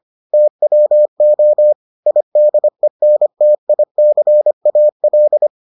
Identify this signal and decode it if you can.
Morse code, heard as a keyed tone, about 25 wpm, 600 Hz. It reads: TWO IDENTICAL